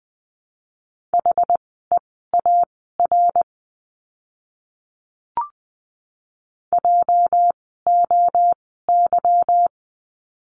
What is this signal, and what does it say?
Morse code, heard as a keyed tone, about 20 wpm, spaced as for 14 wpm, 700 Hz. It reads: HEAR JOY
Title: 206 Lesson 33f - Letter j - Words 20wpm